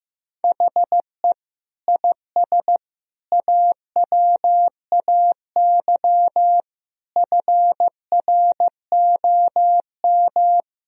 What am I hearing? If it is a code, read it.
Morse code, heard as a keyed tone, about 15 wpm, 700 Hz. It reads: HE IS AWAY FROM
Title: Sentences from Top 1000 Words - Review Part 1 - 15wpm